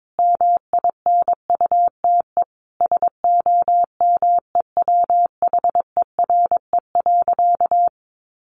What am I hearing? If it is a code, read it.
Morse code, heard as a keyed tone, about 22 wpm, 700 Hz. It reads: MINUTE SOMEWHERE.
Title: Sentences from Top 1400 Words 22wpm